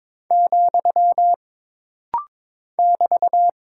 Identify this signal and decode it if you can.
Morse code, heard as a keyed tone, about 22 wpm, 700 Hz. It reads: , <BT>
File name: practice / 22wpm / Single Punctuation-Prosign - Rapid-Fire - 22wpm